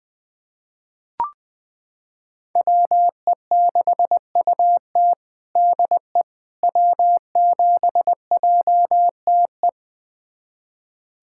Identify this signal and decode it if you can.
Morse code, heard as a keyed tone, about 20 wpm, 700 Hz. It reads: WE6UT DE W7JTE
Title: USA Callsign DE Callsign - 5 Characters - 20wpm